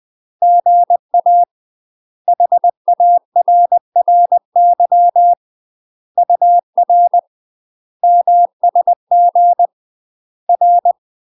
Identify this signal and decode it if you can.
Morse code, heard as a keyed tone, about 20 wpm, 700 Hz. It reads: GA HARRY UR MSG R
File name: UK MASTER QSOs TRAINING / 20WPM / UK MASTER QSOs 0014